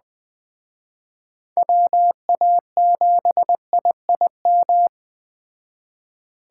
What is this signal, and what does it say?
Morse code, heard as a keyed tone, about 20 wpm, 700 Hz. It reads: WA7IIM
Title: Speed Racing - USA 2x3 Call Signs 17wpm